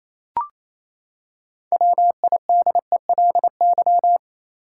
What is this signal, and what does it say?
Morse code, heard as a keyed tone, about 27 wpm, 700 Hz. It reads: WIDELY
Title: Six Letter Words - 28wpm